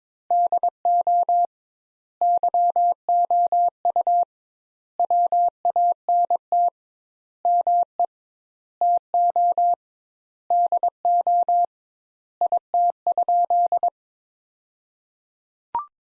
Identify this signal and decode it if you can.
Morse code, heard as a keyed tone, about 21 wpm, 700 Hz. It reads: DO YOU WANT ME TO DO IT?
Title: Sentences Top 200 Words - 22wpm